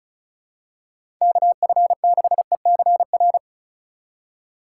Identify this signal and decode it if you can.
Morse code, heard as a keyed tone, about 34 wpm, 690 Hz. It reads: KF6ECR
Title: USA Call Signs - 6 Characters - 2x Character Spacing - 35wpm